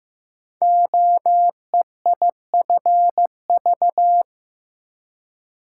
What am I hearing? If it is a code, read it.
Morse code, heard as a keyed tone, about 15 wpm, 700 Hz. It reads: OEIFV
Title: Code Groups - 5 Letters - 15wpm